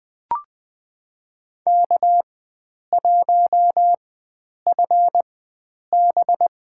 Morse code, heard as a keyed tone, about 21 wpm, spaced as for 9 wpm, 700 Hz. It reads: K1FB
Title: USA Call Signs - 4 Characters - 4x Character Spacing - 20wpm